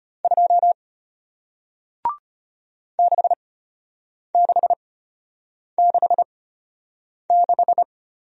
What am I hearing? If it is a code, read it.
Morse code, heard as a keyed tone, about 33 wpm, spaced as for 8 wpm, 700 Hz. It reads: 2 6666